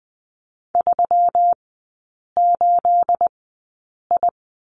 Morse code, heard as a keyed tone, about 20 wpm, spaced as for 8 wpm, 700 Hz. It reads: 38I
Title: 212 Lesson 34e - Number 8 - Three Characters 20wpm